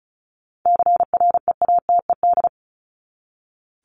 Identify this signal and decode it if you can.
Morse code, heard as a keyed tone, about 34 wpm, 700 Hz. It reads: CREATED
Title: ED Suffix - Top 100 Words with Base Words 35wpm